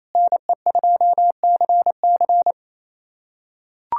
Morse code, heard as a keyed tone, about 28 wpm, 700 Hz. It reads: NE2CC
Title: USA Callsign DE Callsign 28wpm